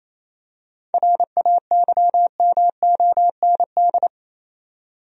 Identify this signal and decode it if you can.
Morse code, heard as a keyed tone, about 28 wpm, 700 Hz. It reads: RAYMOND
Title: USA Common Names - 28wpm